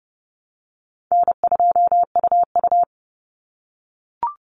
Morse code, heard as a keyed tone, about 30 wpm, 700 Hz. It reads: N2UU
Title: Easy Call Signs - 30wpm